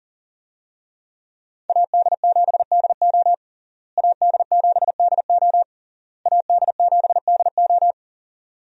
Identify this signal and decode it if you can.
Morse code, heard as a keyed tone, about 39 wpm, 700 Hz. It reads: AD7DO AD7DO AD7DO
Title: USA Call Signs - 5 Characters - Repeated 3x - 40wpm